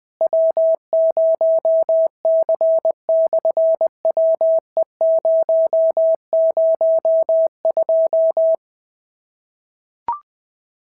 Morse code, heard as a keyed tone, about 20 wpm, 660 Hz. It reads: W0C/WE002
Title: SOTA QSO Practice 20wpm